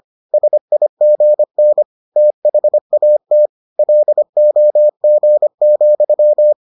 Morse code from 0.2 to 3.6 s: SIGN THAT

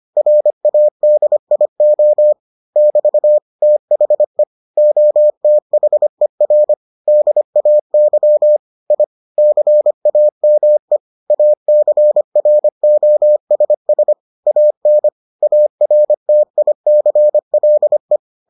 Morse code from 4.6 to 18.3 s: OTHER DAY I CAME ACROSS AN ARTICLE

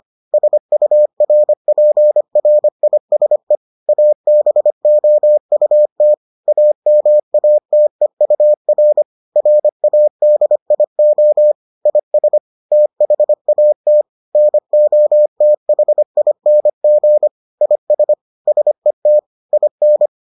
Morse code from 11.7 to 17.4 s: IS THAT NOTHING

